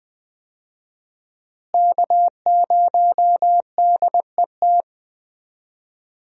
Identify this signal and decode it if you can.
Morse code, heard as a keyed tone, about 20 wpm, 700 Hz. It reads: K0DET
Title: USA Call Signs - 5 Characters - Repeated 3x - 20wpm